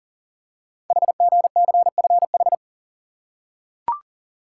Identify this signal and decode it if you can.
Morse code, heard as a keyed tone, about 40 wpm, 700 Hz. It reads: HGCFH
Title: Code Groups - 5 Letters - 40wpm